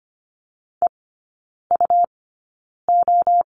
Morse code, heard as a keyed tone, about 24 wpm, 700 Hz. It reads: E U O
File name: course / 25:10wpm / 100 Lesson 17e - Letter u - Three Characters 25wpm